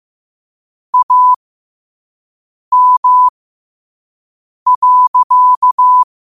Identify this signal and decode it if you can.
Morse code, heard as a keyed tone, about 15 wpm, spaced as for 5 wpm, 1000 Hz. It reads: AM.
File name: qotdfiles / 5wpm